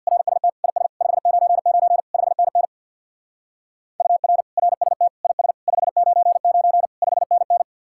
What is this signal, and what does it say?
Morse code, heard as a keyed tone, about 66 wpm, 700 Hz. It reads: RST IS 599 5NN UR RST IS 599 5NN